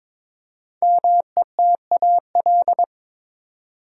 Morse code, heard as a keyed tone, about 22 wpm, 690 Hz. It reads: METAL
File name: 5 Letter Words - First 3 Letters Given - 22wpm